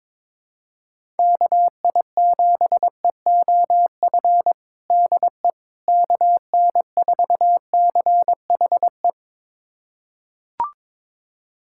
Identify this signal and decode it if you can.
Morse code, heard as a keyed tone, about 21 wpm, 700 Hz. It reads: KI7EOF DE KN4CHE
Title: USA Callsign DE Callsign - 6 Characters - 22wpm